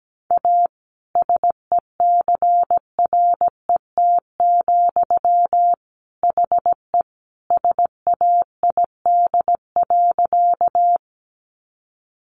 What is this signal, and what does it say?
Morse code, heard as a keyed tone, about 17 wpm, 700 Hz. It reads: A SECRET, HE SAID.
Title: Sentences from Top 1200 Words 17wpm